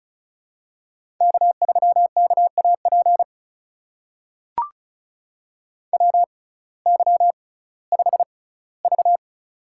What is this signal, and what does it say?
Morse code, heard as a keyed tone, about 36 wpm, spaced as for 19 wpm, 700 Hz. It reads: K3KAP WY5V
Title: USA Call Signs - 6x Character Spacing - 35wpm